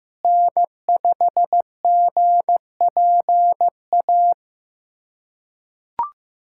Morse code, heard as a keyed tone, about 15 wpm, 700 Hz. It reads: N5GPA
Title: USA Callsign DE Callsign - 5 Characters - 15wpm